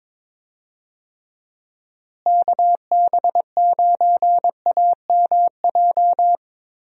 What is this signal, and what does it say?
Morse code, heard as a keyed tone, about 22 wpm, 700 Hz. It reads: KB9AMJ
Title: USA Call Signs - 6 Characters - Repeated 3x - 22wpm